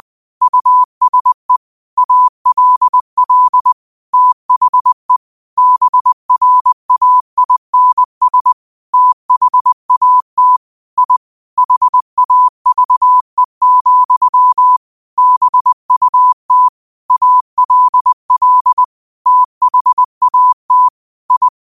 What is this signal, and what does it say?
Morse code, heard as a keyed tone, about 20 wpm, 1000 Hz. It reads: USE ALL THE BRAINS THAT I HAVE, BUT ALL THAT I
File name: qotdfiles / 20wpm